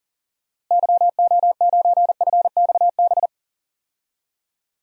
Morse code, heard as a keyed tone, about 40 wpm, 700 Hz. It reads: YO9FXB
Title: DXCC-100 Call Signs - 6 Characters - Repeated 3x - 40wpm